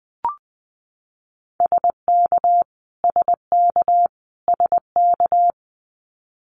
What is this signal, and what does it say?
Morse code, heard as a keyed tone, about 20 wpm, 700 Hz. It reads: SK SK SK
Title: Canadian Provinces and Territories - Repeated 3x - 20wpm